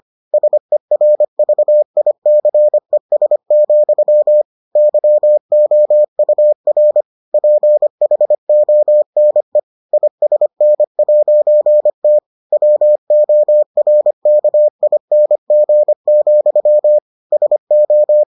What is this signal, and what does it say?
Morse code, heard as a keyed tone, about 25 wpm, 600 Hz. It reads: SERVICES, YOUR PHONE ISN'T WORKING, SO